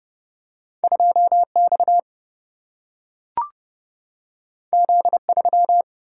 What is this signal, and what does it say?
Morse code, heard as a keyed tone, about 30 wpm, 700 Hz. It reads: 2 X Z 3
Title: Single-Letter Single-Number 30wpm